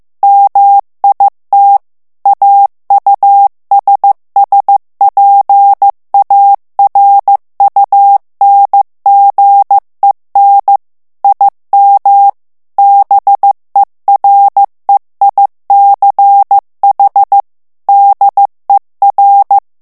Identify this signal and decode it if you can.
Morse code, heard as a keyed tone, about 15 wpm, 800 Hz. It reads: MIT AUSSPARUNGEN IM BEREICH DER
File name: morse mp3 / ALLEMAND texte clair 1 vitesse 14 mots minute